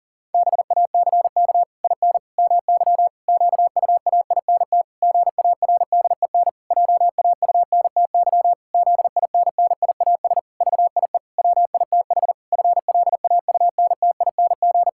Morse code, heard as a keyed tone, about 37 wpm, 700 Hz. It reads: BACK IN MY QUAINT GARDEN JAUNTY ZINNIAS VIE WITH FLAUNTING
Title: Pangram Challenge v2 - 40wpm